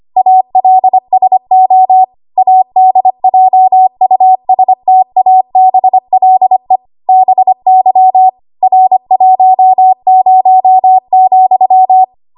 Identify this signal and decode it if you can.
Morse code, heard as a keyed tone, about 25 wpm, 750 Hz. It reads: ALSO ADJUSTABLE BY R10,
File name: Archive / 25 WPM / 25WPM